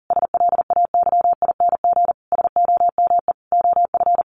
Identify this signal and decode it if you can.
Morse code, heard as a keyed tone, about 40 wpm, 700 Hz. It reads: SLAYING SOME OF
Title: A Sound Of Thunder 40wpm